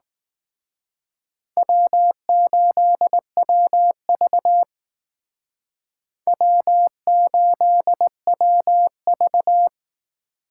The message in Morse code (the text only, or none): W8WV W8WV